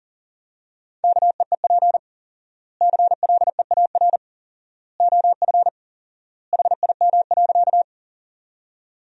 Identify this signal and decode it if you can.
Morse code, heard as a keyed tone, about 40 wpm, 700 Hz. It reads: KEEP CLEAR OF HIM.
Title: 4x Word-Spacing - Sentences from Top 500 Words - 40wpm